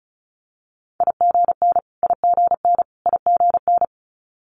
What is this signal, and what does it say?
Morse code, heard as a keyed tone, about 36 wpm, 700 Hz. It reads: IGN IGN IGN
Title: Multi-letter Phonemes Repeated 3x 35wpm